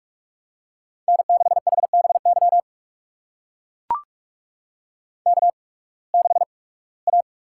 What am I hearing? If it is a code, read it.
Morse code, heard as a keyed tone, about 44 wpm, spaced as for 23 wpm, 700 Hz. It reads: N6HBY K6A
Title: USA Call Signs - 5 Characters - 8x Character Spacing - 45wpm